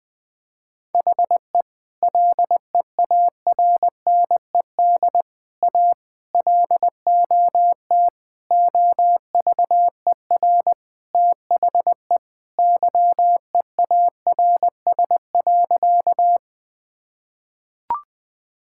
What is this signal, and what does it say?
Morse code, heard as a keyed tone, about 20 wpm, 700 Hz. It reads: HE LEARNED A LOT OVER THE YEARS.
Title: Sentences with 3-Word Phrases - Spoken Beginning - 20wpm